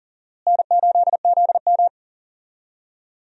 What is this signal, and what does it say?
Morse code, heard as a keyed tone, about 39 wpm, 700 Hz. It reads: N8ZM
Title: USA Call Signs - 4 Characters - Repeated 3x - 40wpm